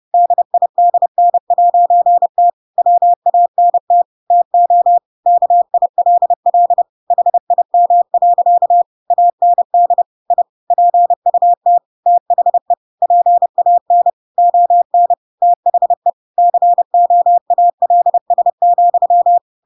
Morse code, heard as a keyed tone, about 30 wpm, 700 Hz. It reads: DIDN'T WANT TO KILL HIM. AND I PUT THE PAN ON THE COALS,
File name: Pinocchio / Pinocchio ch7 30 30 ik1hge